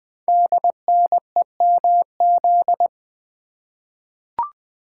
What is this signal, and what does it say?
Morse code, heard as a keyed tone, about 20 wpm, 700 Hz. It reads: DNEMZ
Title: Code Groups - 5 Letters - 20wpm